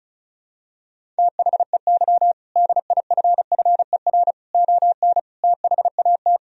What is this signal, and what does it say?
Morse code, heard as a keyed tone, about 34 wpm, 700 Hz. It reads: THEY DIFFER ON THAT
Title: Sentences from Top 1000 Words - Review Part 1 - 35wpm